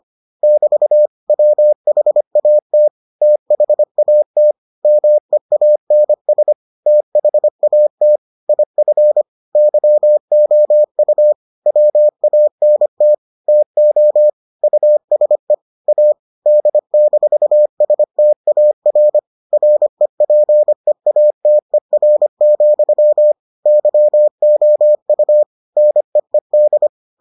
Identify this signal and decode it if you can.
Morse code, heard as a keyed tone, about 25 wpm, 600 Hz. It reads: <BT> WHAT THAT MEANS THAT IF YOU WANT TO USE A D-STAR REPEATER, YOU NEED